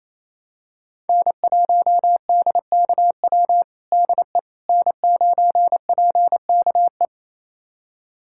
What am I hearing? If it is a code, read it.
Morse code, heard as a keyed tone, about 28 wpm, 700 Hz. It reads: N1DKW DE N9PKE
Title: USA Callsign DE Callsign - 5 Characters - 28wpm